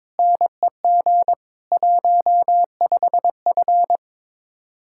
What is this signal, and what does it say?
Morse code, heard as a keyed tone, about 23 wpm, 700 Hz. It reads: NEG 15F
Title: Speed-Racing Common QSO Phrases - 22wpm